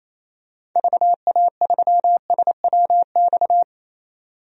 Morse code, heard as a keyed tone, about 28 wpm, 700 Hz. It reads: VA3SWX
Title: DXCC-100 Call Signs - 6 Characters - 28wpm